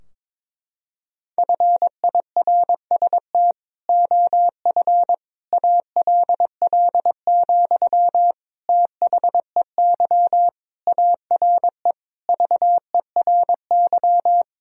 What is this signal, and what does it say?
Morse code, heard as a keyed tone, about 22 wpm, 700 Hz. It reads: FIRST OF ALL, THEY ARE VERY
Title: Sentences from Top 1000 Words - Review Part 4 - 22wpm